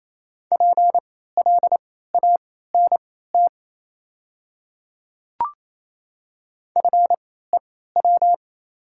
Morse code, heard as a keyed tone, about 28 wpm, spaced as for 16 wpm, 700 Hz. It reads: PLANT FEW
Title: 3x Character Spacing - Top 300 Words - 28wpm